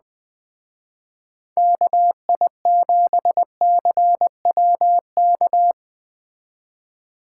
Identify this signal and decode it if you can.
Morse code, heard as a keyed tone, about 20 wpm, 700 Hz. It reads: KI7CWK
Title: Speed Racing - USA 2x3 Call Signs 17wpm